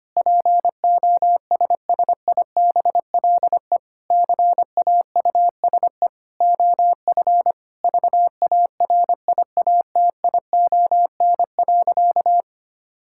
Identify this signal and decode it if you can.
Morse code, heard as a keyed tone, about 25 wpm, 700 Hz. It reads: POSSIBLE CAUSE OF VARIATION.
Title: Sentences from Top 1600 Words 25wpm